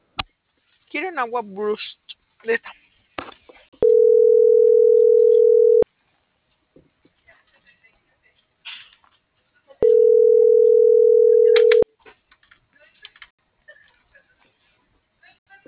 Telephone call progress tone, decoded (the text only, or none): ringback tone